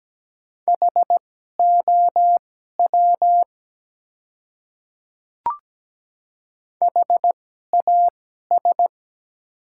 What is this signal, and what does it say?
Morse code, heard as a keyed tone, about 17 wpm, spaced as for 12 wpm, 700 Hz. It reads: HOW HAS
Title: 2x Character Spacing - Top 100 Words - 17wpm